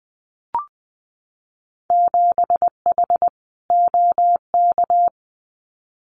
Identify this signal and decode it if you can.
Morse code, heard as a keyed tone, about 20 wpm, 700 Hz. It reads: 7H OK
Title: Winter Field Day Exchanges Encore 20wpm